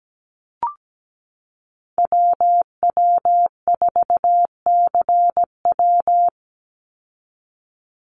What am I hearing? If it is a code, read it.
Morse code, heard as a keyed tone, about 17 wpm, 700 Hz. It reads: WW4CW
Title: USA Call Signs 17wpm